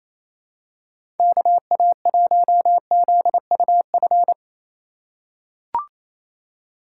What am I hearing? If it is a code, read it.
Morse code, heard as a keyed tone, about 27 wpm, 700 Hz. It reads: KA1ZUF
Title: USA Call Signs - 6 Characters - 2x Character Spacing - 28wpm